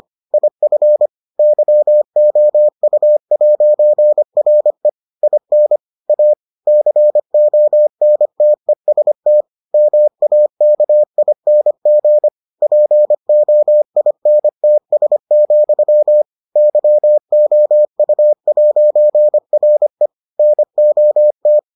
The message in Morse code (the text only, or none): IF YOU'RE IN A CONTEST MAKING POINTS, YOU'RE NOT